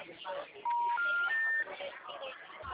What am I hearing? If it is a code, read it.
Special information tone, with speech.